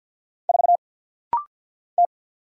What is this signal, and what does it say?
Morse code, heard as a keyed tone, about 50 wpm, 700 Hz. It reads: 4 T
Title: Single Letter-Number - Rapid-Fire 50wpm